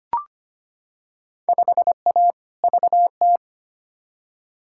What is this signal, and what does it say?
Morse code, heard as a keyed tone, about 25 wpm, 700 Hz. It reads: 5A VT
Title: Field Day Exchanges Encore - 25wpm